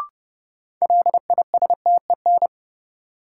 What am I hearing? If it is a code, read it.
Morse code, heard as a keyed tone, about 30 wpm, 700 Hz. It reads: LISTEN
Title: Sentences from Top 400 Words - New Word Drill - 30wpm